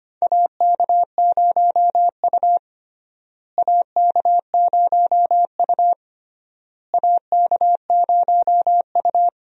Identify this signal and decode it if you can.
Morse code, heard as a keyed tone, about 25 wpm, 700 Hz. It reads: AK0U AK0U AK0U
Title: USA Call Signs - 4 Characters - Repeated 3x - 25wpm